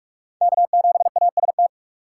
Morse code, heard as a keyed tone, about 42 wpm, 700 Hz. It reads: K7AST